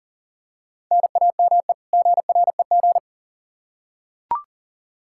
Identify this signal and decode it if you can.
Morse code, heard as a keyed tone, about 40 wpm, 700 Hz. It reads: NAME GREG
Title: Practice QSO - Exchange 1 - Beginning - 40wpm